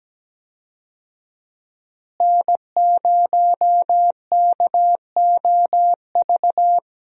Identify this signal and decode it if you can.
Morse code, heard as a keyed tone, about 17 wpm, 700 Hz. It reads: N0KOV
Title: USA Call Signs - 5 Characters - Repeated 3x - 17wpm